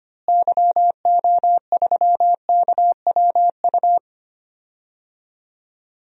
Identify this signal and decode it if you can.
Morse code, heard as a keyed tone, about 25 wpm, 700 Hz. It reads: YO3KWU
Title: DXCC-100 Call Signs - 6 Characters - Repeated 2x - 25wpm